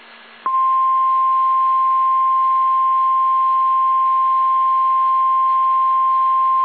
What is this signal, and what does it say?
Emergency Alert System (SAME) tones (the 1050 Hz attention tone). It is heard from 1.0 to 6.0 s.
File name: eas / KSDZKDJL / KSDZKDJL